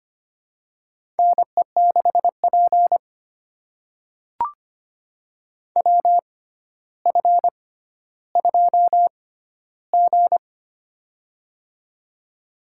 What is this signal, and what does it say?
Morse code, heard as a keyed tone, about 25 wpm, spaced as for 13 wpm, 700 Hz. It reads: NE6P WF2G
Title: USA Call Signs - 4 Characters - 6x Character Spacing - 25wpm